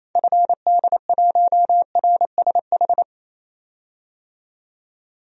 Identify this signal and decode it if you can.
Morse code, heard as a keyed tone, about 28 wpm, 700 Hz. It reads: FD1RSH
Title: DXCC-100 Call Signs - 6 Characters - 28wpm